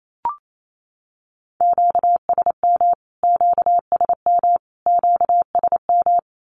Morse code, heard as a keyed tone, about 28 wpm, 700 Hz. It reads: QSM QSM QSM
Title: Q-Codes Repeated 3x 28wpm